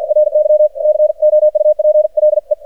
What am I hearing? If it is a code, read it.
Morse code, heard as a keyed tone, about 52 wpm, 610 Hz, with radio noise. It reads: RY COUPLE
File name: archived / CRSnet